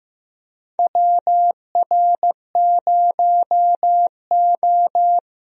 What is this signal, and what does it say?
Morse code, heard as a keyed tone, about 15 wpm, 700 Hz. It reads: WR0O
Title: USA Call Signs - 4 Characters - Repeated 3x - 15wpm